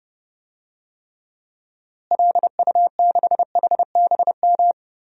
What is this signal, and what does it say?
Morse code, heard as a keyed tone, about 30 wpm, 700 Hz. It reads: LU6HBM
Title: DXCC-100 Call Signs - 6 Characters - Repeated 2x - 30wpm